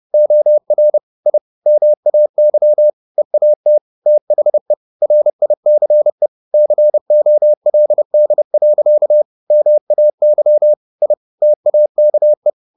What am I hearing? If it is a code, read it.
Morse code, heard as a keyed tone, about 30 wpm, 600 Hz. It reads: OR I MAY EAT THE RICE COLD. MAY I TAKE
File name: the old man and the sea / the old man and the sea 1 30wpm